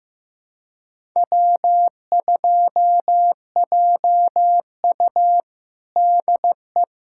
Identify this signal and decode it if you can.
Morse code, heard as a keyed tone, about 15 wpm, 700 Hz. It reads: W2JU DE
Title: USA Callsign DE Callsign - 4 Characters - 15wpm